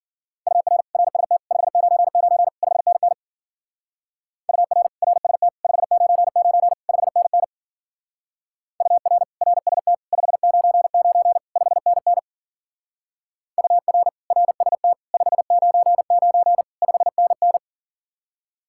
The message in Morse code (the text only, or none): UR RST 599 5NN UR RST 599 5NN UR RST 599 5NN UR RST 599 5NN